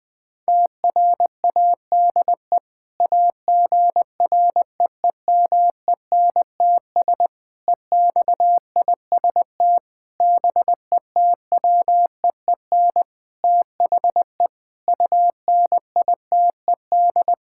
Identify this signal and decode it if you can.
Morse code, heard as a keyed tone, about 20 wpm, 700 Hz. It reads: TRADE AGREEMENTS EXIST BETWEEN THE UNITED
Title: Sentences with 5-Word Phrases - 20wpm